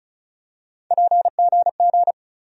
Morse code, heard as a keyed tone, about 36 wpm, 700 Hz. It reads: PGG
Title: USA Call Sign Postfixes - 3 Letters 35wpm